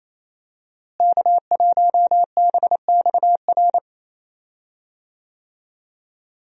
Morse code, heard as a keyed tone, about 28 wpm, 700 Hz. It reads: K1BXR
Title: USA Call Signs - 5 Characters - Repeated 3x - 28wpm